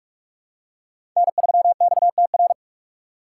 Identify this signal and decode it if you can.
Morse code, heard as a keyed tone, about 42 wpm, 700 Hz. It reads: N3XTR
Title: USA Call Signs - 5 Characters - 4x Character Spacing - 45wpm